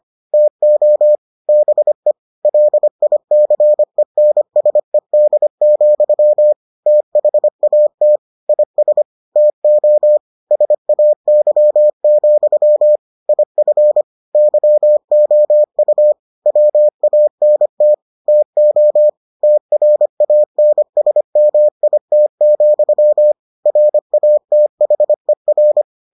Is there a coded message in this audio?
Yes: Morse code from 0.3 to 25.8 s, heard as a keyed tone, about 25 wpm, 600 Hz. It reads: TO BE LICENSED, THAT IS TO SAY, IF YOU WANT TO TRANSMIT, RATHER